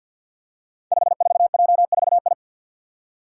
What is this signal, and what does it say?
Morse code, heard as a keyed tone, about 52 wpm, 700 Hz. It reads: 54J4I
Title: Five Challenging Letter-Number Combinations 50wpm